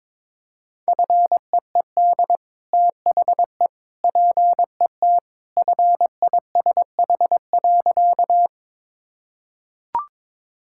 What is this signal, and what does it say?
Morse code, heard as a keyed tone, about 21 wpm, 700 Hz. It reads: FEED THE PET FISH.
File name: practice / 22wpm / Easy Sentences with Hints - 22wpm